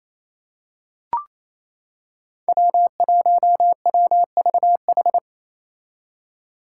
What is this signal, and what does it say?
Morse code, heard as a keyed tone, about 28 wpm, 700 Hz. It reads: W1WVH
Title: Five Challenging Letter-Number Combinations 28wpm